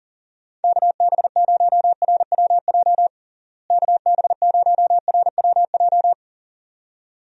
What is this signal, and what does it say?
Morse code, heard as a keyed tone, about 40 wpm, 700 Hz. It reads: KB0RWJ KB0RWJ
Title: USA Call Signs - 6 Characters - Repeated 2x - 40wpm